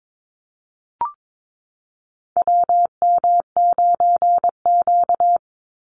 Morse code, heard as a keyed tone, about 22 wpm, 700 Hz. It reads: WM9Q
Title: Call Signs Repeated 3x - 22wpm